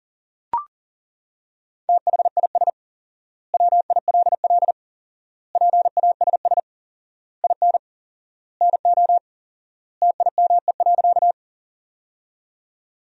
Morse code, heard as a keyed tone, about 40 wpm, 700 Hz. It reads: THIS WILL PASS IN NO TIME.
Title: 4x Word-Spacing - Sentences from Top 400 Words - 40wpm